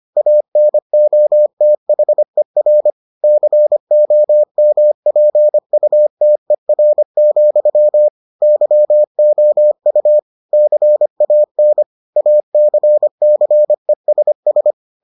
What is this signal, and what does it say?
Morse code, heard as a keyed tone, about 25 wpm, 600 Hz. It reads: ANOTHER COMPUTER, YOU CAN ACCESS